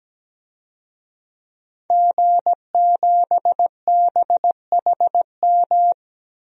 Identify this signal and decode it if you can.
Morse code, heard as a keyed tone, about 17 wpm, 700 Hz. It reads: G7BHM
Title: DXCC-100 Call Signs - 5 Characters - Repeated 3x - 17wpm